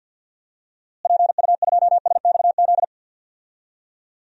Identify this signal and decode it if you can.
Morse code, heard as a keyed tone, about 49 wpm, 700 Hz. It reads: PU2SXZ
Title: DXCC-100 Call Signs - 6 Characters - Repeated 2x - 50wpm